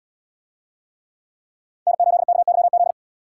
Morse code, heard as a keyed tone, about 71 wpm, 700 Hz. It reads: N8KYZ